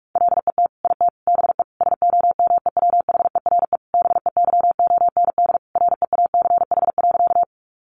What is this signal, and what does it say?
Morse code, heard as a keyed tone, about 44 wpm, 700 Hz. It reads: LET IT BE SOMEWHERE BEYOND REACH.
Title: Sentences from Top 1400 Words 45wpm